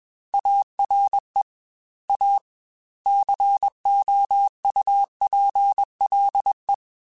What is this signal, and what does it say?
Morse code, heard as a keyed tone, about 21 wpm, 800 Hz. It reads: ARE A COUPLE
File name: Technical Subjects / Technical Subjects-mp3s / Sw Pwr Sup @20